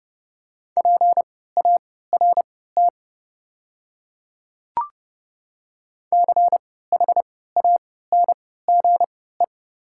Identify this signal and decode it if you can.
Morse code, heard as a keyed tone, about 30 wpm, spaced as for 17 wpm, 700 Hz. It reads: PART CHANGE
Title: 3x Character Spacing - Top 200 Words - 30wpm